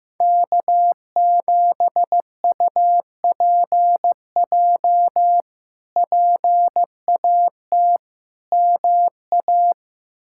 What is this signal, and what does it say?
Morse code, heard as a keyed tone, about 16 wpm, 700 Hz. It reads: K7UPJ PAT MA
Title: K1USN SST QSO Practice - 15wpm